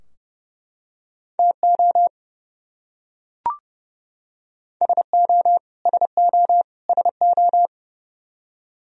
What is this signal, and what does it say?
Morse code, heard as a keyed tone, about 30 wpm, 700 Hz. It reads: TO SO SO SO
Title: Top 100 Words Repeated 30wpm